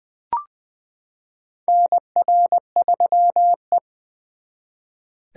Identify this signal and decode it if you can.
Morse code, heard as a keyed tone, about 20 wpm, 700 Hz. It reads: NR3E
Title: 2023 SOTA Chaser Call Signs - 20wpm